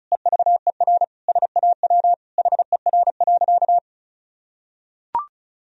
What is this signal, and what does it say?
Morse code, heard as a keyed tone, about 36 wpm, 700 Hz. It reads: EVER SAW HER.
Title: Sentences from Top 1000 Words - Review Part 2 - 35wpm